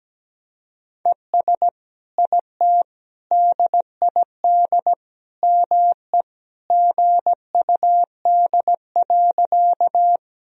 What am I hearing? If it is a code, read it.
Morse code, heard as a keyed tone, about 17 wpm, 700 Hz. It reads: ES IT DID ME GUD.
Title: Sentences from Top 200 Words and Abbreviations 17wpm